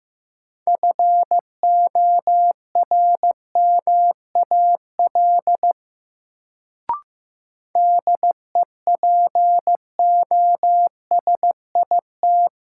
Morse code, heard as a keyed tone, about 15 wpm, 700 Hz. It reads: FORMAL DEPOSIT
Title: Sentences from Top 1300 Words - New Word Drill Alt - 15wpm